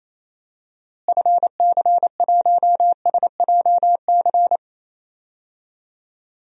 Morse code, heard as a keyed tone, about 28 wpm, 700 Hz. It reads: FC1SJC
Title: DXCC-100 Call Signs - 6 Characters - Repeated 3x - 28wpm